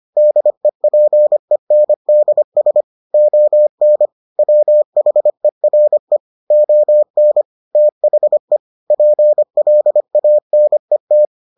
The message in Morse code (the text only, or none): DEPENDS ON WHERE ON THE PLANET